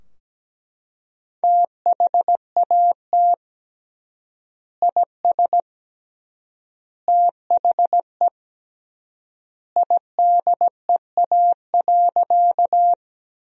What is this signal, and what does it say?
Morse code, heard as a keyed tone, about 17 wpm, 700 Hz. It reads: THAT IS THE IDEA.
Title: 3x Word-Spacing - Sentences from Top 300 Words - 17wpm